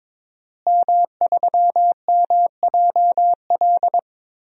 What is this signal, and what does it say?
Morse code, heard as a keyed tone, about 22 wpm, 700 Hz. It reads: M3MJL
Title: DXCC-100 Call Signs - 5 Characters - Repeated 2x - 22wpm